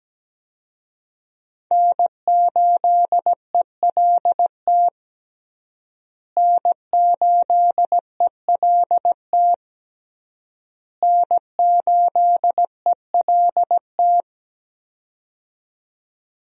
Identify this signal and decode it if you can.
Morse code, heard as a keyed tone, about 17 wpm, 700 Hz. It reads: N8ELT N8ELT N8ELT
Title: USA Call Signs - 5 Characters - Repeated 3x - 17wpm